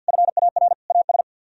Morse code, heard as a keyed tone, about 66 wpm, spaced as for 58 wpm, 700 Hz. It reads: FAR AS
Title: Speed-Racing - Sets of 3 Words 50wpm